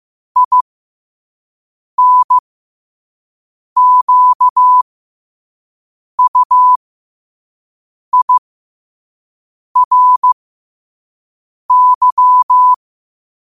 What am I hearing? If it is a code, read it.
Morse code, heard as a keyed tone, about 15 wpm, spaced as for 5 wpm, 1000 Hz. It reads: INQUIRY
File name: qotdfiles / 5wpm